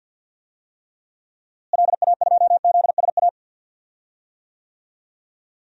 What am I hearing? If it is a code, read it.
Morse code, heard as a keyed tone, about 46 wpm, 700 Hz. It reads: LA2ZSA
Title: DXCC-100 Call Signs - 6 Characters - Repeated 2x - 50wpm